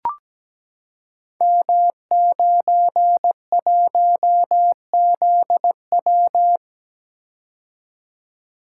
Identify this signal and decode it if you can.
Morse code, heard as a keyed tone, about 17 wpm, 700 Hz. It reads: M91ZW
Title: Code Groups - 5 Letter:Numbers - 17wpm